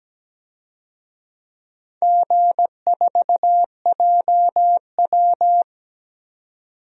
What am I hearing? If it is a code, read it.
Morse code, heard as a keyed tone, about 17 wpm, 700 Hz. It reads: G4JW
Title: DXCC-100 Call Signs - 4 Characters - 17wpm